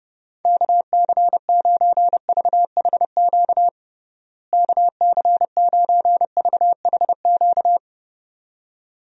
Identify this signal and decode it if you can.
Morse code, heard as a keyed tone, about 32 wpm, 700 Hz. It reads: KC9VHQ KC9VHQ
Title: USA Call Signs - 6 Characters - Repeated 2x - 30wpm